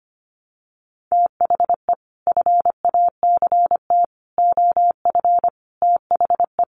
Morse code, heard as a keyed tone, about 25 wpm, 700 Hz. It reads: THE FACT OF THE
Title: Sets of 5 Words 25wpm